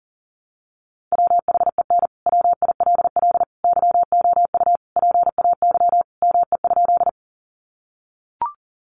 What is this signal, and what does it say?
Morse code, heard as a keyed tone, about 40 wpm, 700 Hz. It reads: WHEN WILL YOU PAY ME?
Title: Sentences from Top 700 Words 40wpm